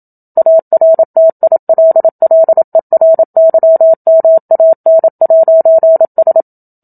Morse code, heard as a keyed tone, about 27 wpm, 650 Hz. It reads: ARTILLERYMAN'S
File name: Book Two / 01-Chapter 1 - 27 WPM